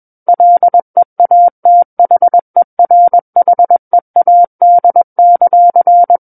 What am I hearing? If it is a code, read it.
Morse code, heard as a keyed tone, about 21 wpm, 700 Hz. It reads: LEATHERHEAD;
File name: Book One / 12-Chapter 12 - 21 WPM